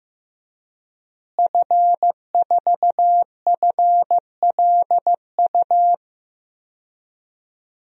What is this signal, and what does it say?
Morse code, heard as a keyed tone, about 16 wpm, 700 Hz. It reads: F4FLU
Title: DXCC-100 Call Signs - 5 Characters - Repeated 3x - 15wpm